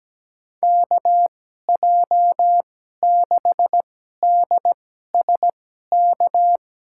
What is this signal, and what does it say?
Morse code, heard as a keyed tone, about 17 wpm, spaced as for 12 wpm, 700 Hz. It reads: KJ6DSK
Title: USA Call Signs - 2x Character Spacing - 17wpm